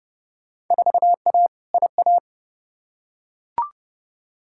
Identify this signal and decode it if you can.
Morse code, heard as a keyed tone, about 32 wpm, 700 Hz. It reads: 4A IA
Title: Field Day Exchanges - 30wpm